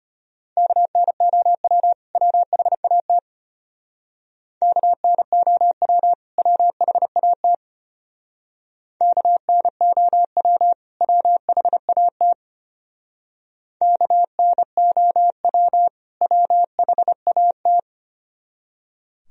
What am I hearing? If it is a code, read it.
Morse code, heard as a keyed tone, about 30 wpm, 700 Hz. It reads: KNOW WHAT KNOW WHAT KNOW WHAT KNOW WHAT
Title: Speed-Racing - Sets of 2 Words 25wpm